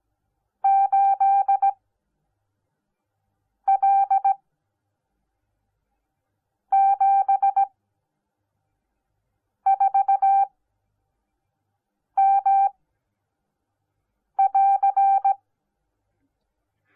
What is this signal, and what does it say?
Morse code, heard as a keyed tone, about 17 wpm, spaced as for 4 wpm, 780 Hz. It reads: 8L74M<AR>